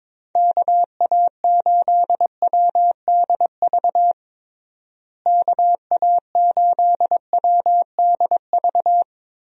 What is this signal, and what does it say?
Morse code, heard as a keyed tone, about 21 wpm, 700 Hz. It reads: KA8WDV KA8WDV
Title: USA Call Signs - 6 Characters - Repeated 3x - 22wpm